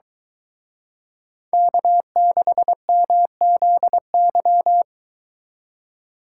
Morse code, heard as a keyed tone, about 23 wpm, 700 Hz. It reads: K6MZY